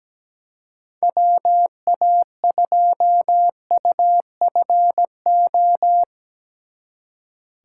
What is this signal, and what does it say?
Morse code, heard as a keyed tone, about 18 wpm, 700 Hz. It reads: WA2UFO
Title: Speed Racing - USA 2x3 Call Signs 17wpm